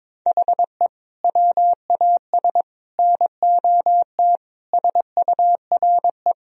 Morse code, heard as a keyed tone, about 22 wpm, 700 Hz. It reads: HE WAS NOT SURE
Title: Sentences from Top 1000 Words - Review Part 1 - 22wpm